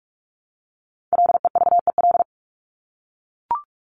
Morse code, heard as a keyed tone, about 44 wpm, 700 Hz. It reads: LEVEL
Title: Top 501-750 Words 45wpm